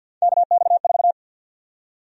Morse code, heard as a keyed tone, about 52 wpm, 700 Hz. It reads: KX4
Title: USA Call Sign Prefixes - 50wpm